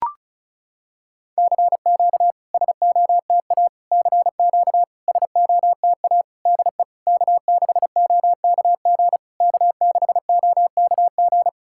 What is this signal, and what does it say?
Morse code, heard as a keyed tone, about 36 wpm, 690 Hz. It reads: CQ SOTA CQ SOTA DE K6OKG K6OKG
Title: SOTA QSO Practice 35wpm